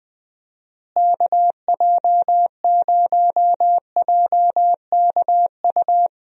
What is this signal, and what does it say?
Morse code, heard as a keyed tone, about 20 wpm, 700 Hz. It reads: KJ0JKU
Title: USA Call Signs - 6 Characters - Repeated 2x - 20wpm